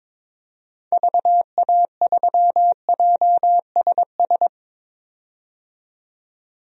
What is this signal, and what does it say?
Morse code, heard as a keyed tone, about 22 wpm, 700 Hz. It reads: VA3JSS
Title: DXCC-100 Call Signs - 6 Characters - 22wpm